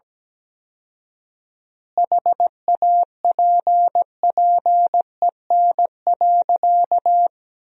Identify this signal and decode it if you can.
Morse code, heard as a keyed tone, about 17 wpm, 700 Hz. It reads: HAPPEN.
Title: 4x Word-Spacing - Sentences from Top 400 Words - 17wpm